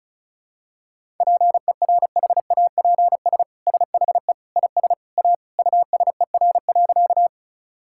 Morse code, heard as a keyed tone, about 34 wpm, 690 Hz. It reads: PERHAPS SHE IS A USER.
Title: Sentences from Top 1000 Words - Review Part 5 - 35wpm